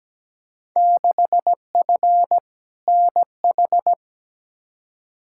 Morse code, heard as a keyed tone, about 17 wpm, 700 Hz. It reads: 6F NH
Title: Field Day Exchanges Encore - 17wpm